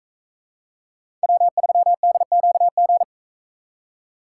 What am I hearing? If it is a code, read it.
Morse code, heard as a keyed tone, about 42 wpm, 700 Hz. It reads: W3DQG